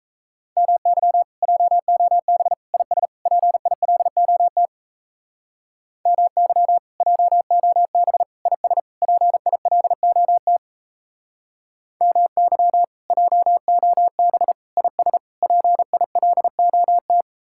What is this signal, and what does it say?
Morse code, heard as a keyed tone, about 35 wpm, 700 Hz. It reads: MY JOB IS PILOT MY JOB IS PILOT MY JOB IS PILOT